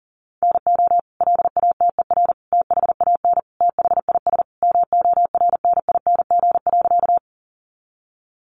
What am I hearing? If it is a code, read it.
Morse code, heard as a keyed tone, about 40 wpm, 700 Hz. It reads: NO LATER THAN THIS MORNING.
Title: Sentences from Top 700 Words 40wpm